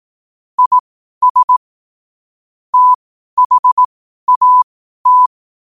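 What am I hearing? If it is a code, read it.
Morse code, heard as a keyed tone, about 18 wpm, spaced as for 12 wpm, 1000 Hz. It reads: IS THAT
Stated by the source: Morse code at 12 wpm.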